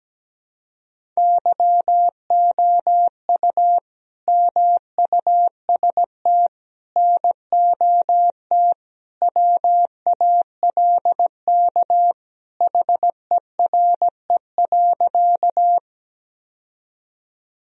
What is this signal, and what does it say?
Morse code, heard as a keyed tone, about 17 wpm, 700 Hz. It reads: YOU MUST NOT WALK HERE.
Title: Sentences from Top 1000 Words - Review Part 1 - 17wpm